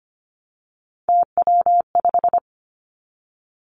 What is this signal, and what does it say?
Morse code, heard as a keyed tone, about 25 wpm, 700 Hz. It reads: TW5
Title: Three Challenging Letter-Number Combinations 25wpm